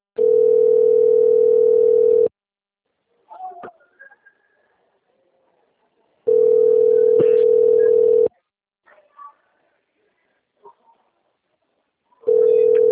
Ringback tone (North America), 440 + 480 Hz.
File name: rg-1600-unknown- - - .13733